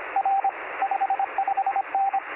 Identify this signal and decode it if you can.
Morse code, heard as a keyed tone, about 24 wpm, 780 Hz, with radio noise. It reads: R 55N
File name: sm7lcb / 0319z yl2ao 23cm